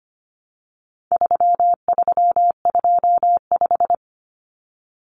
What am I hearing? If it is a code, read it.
Morse code, heard as a keyed tone, about 25 wpm, 700 Hz. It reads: 3325
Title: Common Cut Numbers - 4-Digit 25wpm